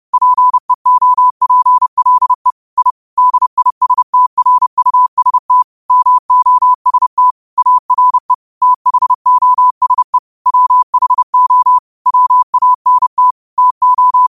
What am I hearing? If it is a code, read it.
Morse code, heard as a keyed tone, about 30 wpm, 1000 Hz. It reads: PEOPLE I DISTRUST MOST ARE THOSE WHO WANT TO
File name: qotdfiles / 30wpm